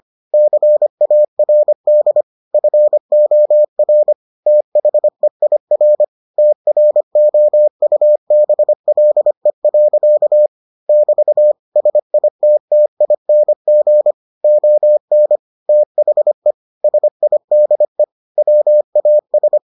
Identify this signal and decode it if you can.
Morse code, heard as a keyed tone, about 25 wpm, 600 Hz. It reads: CARD FOR THEIR TROUBLE. <BT> SITTING ON THE SIDE WAS